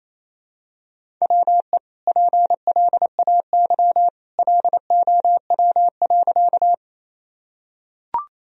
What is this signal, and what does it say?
Morse code, heard as a keyed tone, about 28 wpm, 700 Hz. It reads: WE PLAY LOW.
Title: Sentences from Top 1000 Words - Review Part 1 - 28wpm